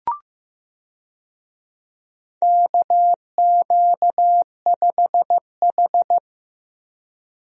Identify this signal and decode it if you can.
Morse code, heard as a keyed tone, about 15 wpm, 700 Hz. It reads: KQ5H
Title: USA Call Signs - 4 Characters - Repeated 3x - 15wpm